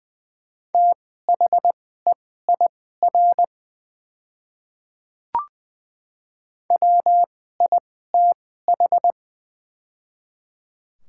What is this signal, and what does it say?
Morse code, heard as a keyed tone, about 20 wpm, spaced as for 15 wpm, 690 Hz. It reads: THEIR WITH
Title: 2x Character Spacing - Top 100 Words - 20wpm